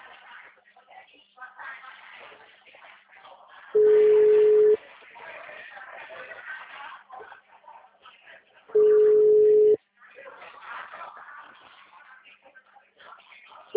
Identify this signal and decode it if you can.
Ringback tone (Europe, 425 Hz), with speech.
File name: out- -2021- - - .97773